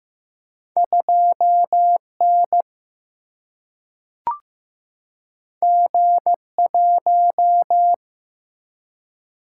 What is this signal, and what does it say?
Morse code, heard as a keyed tone, about 15 wpm, 700 Hz. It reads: 2N G1
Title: Single-Letter Single-Number 15wpm